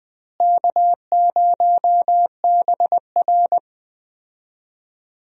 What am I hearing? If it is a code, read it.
Morse code, heard as a keyed tone, about 21 wpm, 700 Hz. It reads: K0BR
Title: USA Call Signs - 4 Characters - Repeated 2x - 20wpm